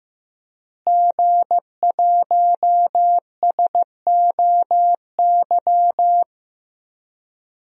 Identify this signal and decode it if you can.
Morse code, heard as a keyed tone, about 15 wpm, 700 Hz. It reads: G1SOY
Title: DXCC-100 Call Signs - 5 Characters - Repeated 2x - 15wpm